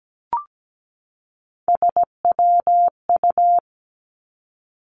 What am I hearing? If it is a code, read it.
Morse code, heard as a keyed tone, about 17 wpm, 700 Hz. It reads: SWU
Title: Three Challenging Letter-Number Combinations 17wpm